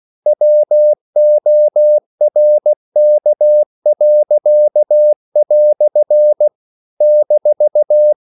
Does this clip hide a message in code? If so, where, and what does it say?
Yes: Morse code from 0.3 to 8.1 s, heard as a keyed tone, about 16 wpm, 600 Hz. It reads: WORK." -